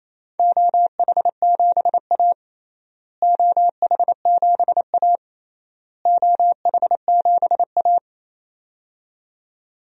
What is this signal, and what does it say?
Morse code, heard as a keyed tone, about 27 wpm, 700 Hz. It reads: OH7A OH7A OH7A
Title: DXCC-100 Call Signs - 4 Characters - Repeated 3x - 28wpm